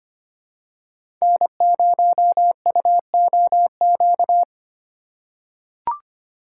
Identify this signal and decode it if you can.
Morse code, heard as a keyed tone, about 25 wpm, 700 Hz. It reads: N0UOQ
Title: USA Call Signs - 5 Characters - 4x Character Spacing - 25wpm